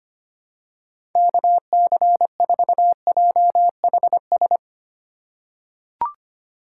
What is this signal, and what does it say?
Morse code, heard as a keyed tone, about 25 wpm, 700 Hz. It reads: KC4JHS
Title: USA Call Signs - 6 Characters - 4x Character Spacing - 25wpm